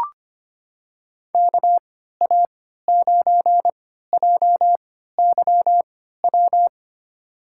Morse code, heard as a keyed tone, about 25 wpm, spaced as for 14 wpm, 700 Hz. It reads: KA9JYW
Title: USA Call Signs - 6 Characters - 3x Character Spacing - 25wpm